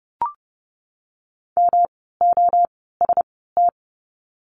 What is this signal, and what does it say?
Morse code, heard as a keyed tone, about 30 wpm, spaced as for 17 wpm, 700 Hz. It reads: MOST
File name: course / 30:10wpm / 237 Lesson 38f - Letter q - Words 30wpm